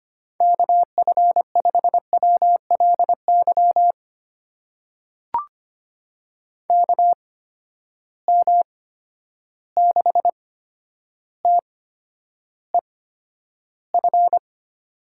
Morse code, heard as a keyed tone, about 25 wpm, spaced as for 11 wpm, 700 Hz. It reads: KF5WLY KM6TEF
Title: USA Call Signs - 6 Characters - 8x Character Spacing - 25wpm